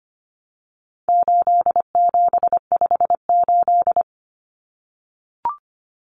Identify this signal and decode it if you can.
Morse code, heard as a keyed tone, about 25 wpm, 700 Hz. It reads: 8758
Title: Numbers - 4-Digit 25wpm